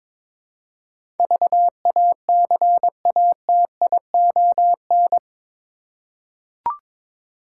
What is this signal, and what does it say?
Morse code, heard as a keyed tone, about 21 wpm, 700 Hz. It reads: VACATION
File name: practice / 22wpm / Eight Letter Words - 22wpm